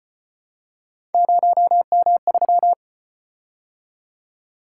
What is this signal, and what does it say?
Morse code, heard as a keyed tone, about 33 wpm, 700 Hz. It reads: 0M3